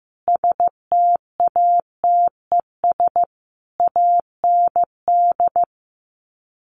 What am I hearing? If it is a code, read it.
Morse code, heard as a keyed tone, about 15 wpm, 700 Hz. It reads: STATES AND
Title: Sets of 5 Words 15wpm